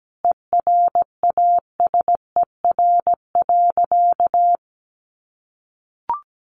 Morse code, heard as a keyed tone, about 17 wpm, 700 Hz. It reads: ERASER.
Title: Additional Easy Phrases 17wpm